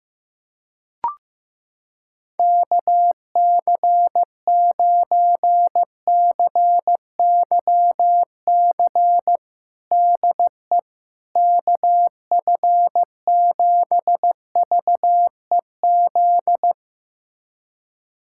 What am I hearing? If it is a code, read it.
Morse code, heard as a keyed tone, about 15 wpm, 700 Hz. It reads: KC9CYC DE KF7VEZ
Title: USA Callsign DE Callsign 15wpm